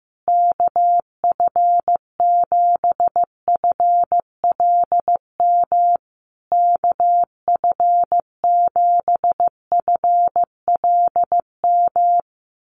Morse code, heard as a keyed tone, about 15 wpm, 700 Hz. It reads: KF7FLM KF7FLM
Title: USA Call Signs Repeated 3x 15wpm